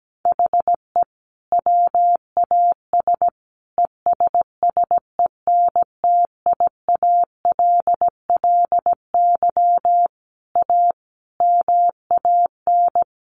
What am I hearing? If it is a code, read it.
Morse code, heard as a keyed tone, about 17 wpm, 700 Hz. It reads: HE WAS ESSENTIALLY A MAN
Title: Sentences from Top 1300 Words 17wpm